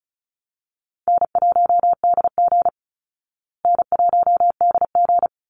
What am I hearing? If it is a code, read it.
Morse code, heard as a keyed tone, about 34 wpm, 700 Hz. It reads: N1DG N1DG
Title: Easy Call Signs Repeated 2x - 35wpm